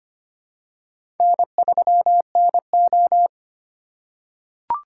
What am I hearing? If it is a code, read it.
Morse code, heard as a keyed tone, about 25 wpm, 700 Hz. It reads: N3NO
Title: USA Call Signs - 4 Characters - 3x Character Spacing - 25wpm